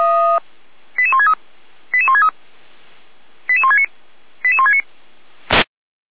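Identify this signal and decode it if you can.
Five-tone selective call (ZVEI), heard as quick runs of five tones, tones of about 70 ms. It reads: 88163 88163 88169 88169
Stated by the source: ZVEI five-tone selective call received on an SDR.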